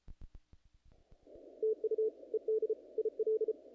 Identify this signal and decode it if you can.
Morse code, heard as a keyed tone, about 34 wpm, 470 Hz. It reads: TU ED IL
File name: contests / naqp-cw-k5tr-op-k5pi- -1900